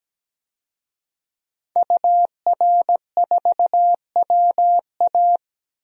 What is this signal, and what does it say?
Morse code, heard as a keyed tone, about 17 wpm, 700 Hz. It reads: UR4WA
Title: DXCC-100 Call Signs - 5 Characters - 17wpm